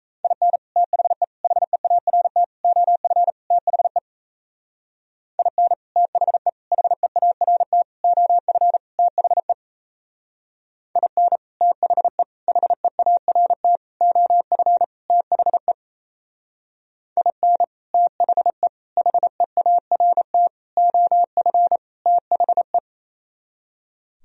Morse code, heard as a keyed tone, about 34 wpm, 700 Hz. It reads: IN THE HEART OF THE IN THE HEART OF THE IN THE HEART OF THE IN THE HEART OF THE
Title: Speed-Racing - Sets of 5 Words 28wpm